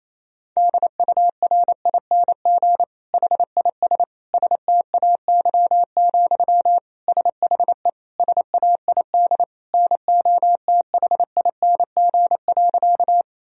Morse code, heard as a keyed tone, about 28 wpm, 700 Hz. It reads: DURING HIS STAY, SHE SAID NOTHING.
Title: Sentences from Top 1000 Words - Review Part 2 - 28wpm